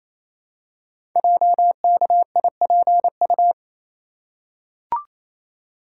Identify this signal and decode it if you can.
Morse code, heard as a keyed tone, about 29 wpm, 700 Hz. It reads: JKIPU
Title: Code Groups - 5 Letters - 28wpm